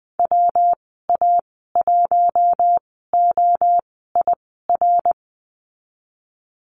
Morse code, heard as a keyed tone, about 20 wpm, spaced as for 15 wpm, 700 Hz. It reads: WA1OIR
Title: 140 Lesson 23g - Letter f - Call Signs 20wpm